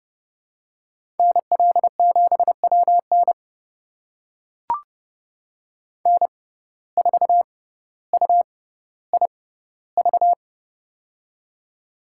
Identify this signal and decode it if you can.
Morse code, heard as a keyed tone, about 30 wpm, spaced as for 15 wpm, 700 Hz. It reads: NL7WN N4UIV
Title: USA Call Signs - 5 Characters - 6x Character Spacing - 30wpm